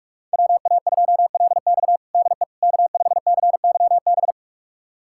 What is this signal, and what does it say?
Morse code, heard as a keyed tone, about 46 wpm, 700 Hz. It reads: WA2LX DE K5CYB
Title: USA Callsign DE Callsign - 5 Characters - 45wpm